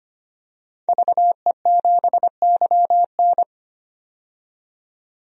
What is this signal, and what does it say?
Morse code, heard as a keyed tone, about 25 wpm, 700 Hz. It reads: VE7YN
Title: DXCC-100 Call Signs - 5 Characters - 25wpm